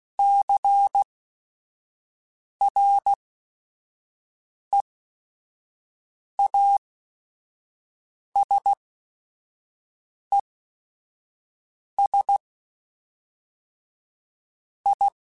Morse code, heard as a keyed tone, about 16 wpm, spaced as for 5 wpm, 800 Hz. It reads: CREASES I
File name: Technical Subjects / Technical Subjects-mp3s / XC @5